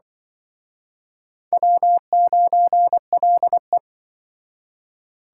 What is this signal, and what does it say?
Morse code, heard as a keyed tone, about 25 wpm, 700 Hz. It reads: W9LE
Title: Speed Racing - USA Call Signs 20wpm